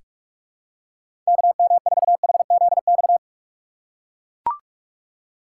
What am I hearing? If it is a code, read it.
Morse code, heard as a keyed tone, about 42 wpm, 700 Hz. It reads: KM4HZX
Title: USA Call Signs - 6 Characters - 4x Character Spacing - 45wpm